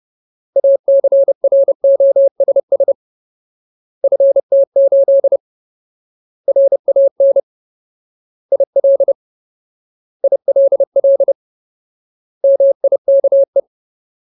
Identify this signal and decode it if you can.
Morse code, heard as a keyed tone, about 30 wpm, 550 Hz. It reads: ACROSS FT8 RAN IL ILL MIKE
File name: Ultimate Word List / 30WPM Ultimate